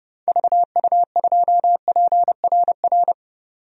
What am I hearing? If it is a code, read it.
Morse code, heard as a keyed tone, about 32 wpm, 700 Hz. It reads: VU2PRR
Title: DXCC-100 Call Signs - 6 Characters - 30wpm